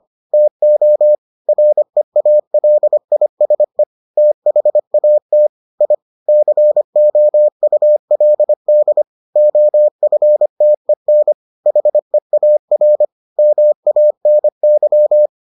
Morse code, heard as a keyed tone, about 25 wpm, 600 Hz. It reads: TO REALISE THAT I COULD OFTEN HEAR MANY